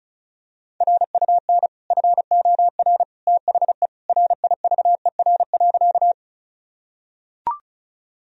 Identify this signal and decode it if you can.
Morse code, heard as a keyed tone, about 34 wpm, 700 Hz. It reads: RUN FOR THE RIVER.
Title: Sentences from Top 1000 Words - Review Part 1 - 35wpm